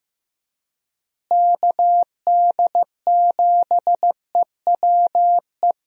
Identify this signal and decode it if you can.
Morse code, heard as a keyed tone, about 15 wpm, 700 Hz. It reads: KD7EWE
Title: USA Call Signs - 6 Characters - Repeated 3x - 15wpm